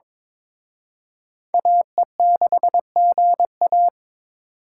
Morse code, heard as a keyed tone, about 26 wpm, 700 Hz. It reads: AE6GA
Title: Speed Racing - USA 2x2 Call Signs 22wpm